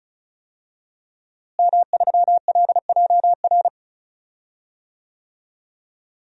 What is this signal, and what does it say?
Morse code, heard as a keyed tone, about 34 wpm, 700 Hz. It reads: M3LJR
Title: DXCC-100 Call Signs - 5 Characters - Repeated 3x - 35wpm